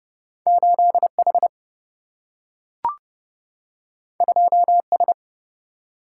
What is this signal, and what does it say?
Morse code, heard as a keyed tone, about 30 wpm, 700 Hz. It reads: 8 H 2 S
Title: Single-Letter Single-Number 30wpm